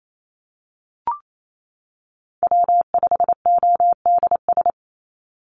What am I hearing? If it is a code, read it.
Morse code, heard as a keyed tone, about 28 wpm, 700 Hz. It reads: W5ODS
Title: NA SOTA Chaser Callsigns 28wpm